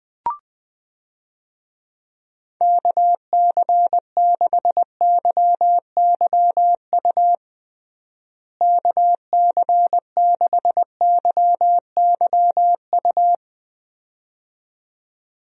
Morse code, heard as a keyed tone, about 20 wpm, 700 Hz. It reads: KC6YYU KC6YYU
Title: USA Call Signs - 6 Characters - Repeated 2x - 20wpm